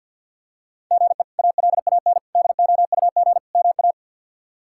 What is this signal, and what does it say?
Morse code, heard as a keyed tone, about 49 wpm, 700 Hz. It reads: GE ALAN DOUG MA
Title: K1USN SST QSO Practice - 50wpm